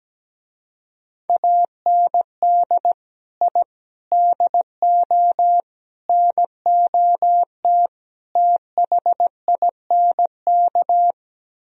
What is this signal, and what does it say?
Morse code, heard as a keyed tone, about 17 wpm, 700 Hz. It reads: AND I DO NOT THINK
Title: Speed-Racing - Sets of 5 Words 17wpm